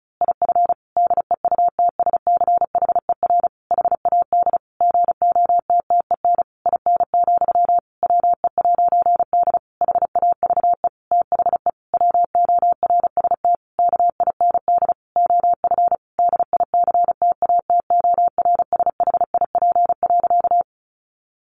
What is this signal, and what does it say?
Morse code, heard as a keyed tone, about 34 wpm, 700 Hz. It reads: IF DEUTSCHER HAD GOTTEN IN, WE'D HAVE THE WORST KIND OF DICTATORSHIP.
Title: A Sound Of Thunder 35wpm